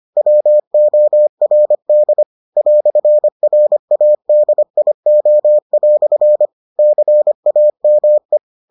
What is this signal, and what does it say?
Morse code, heard as a keyed tone, about 25 wpm, 600 Hz. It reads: WORD "RADIO" CAME